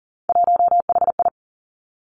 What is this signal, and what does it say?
Morse code, heard as a keyed tone, about 41 wpm, 700 Hz. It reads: 1HI
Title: Three Challenging Letter-Number Combinations 40wpm